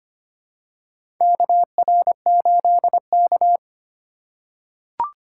Morse code, heard as a keyed tone, about 25 wpm, 700 Hz. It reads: KR8K
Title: USA Call Signs - 4 Characters - 4x Character Spacing - 25wpm